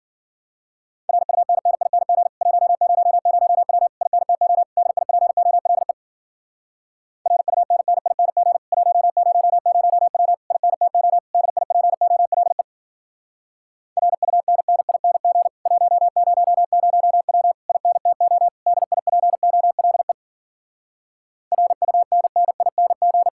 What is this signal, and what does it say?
Morse code, heard as a keyed tone, about 52 wpm, 700 Hz. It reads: RUNNING 100W INTO DIPOLE RUNNING 100W INTO DIPOLE RUNNING 100W INTO DIPOLE RUNNING